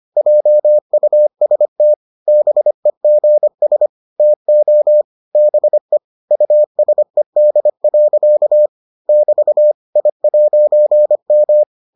Morse code, heard as a keyed tone, about 25 wpm, 600 Hz. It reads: JUST BEGS TO BE USED. <BT> I'M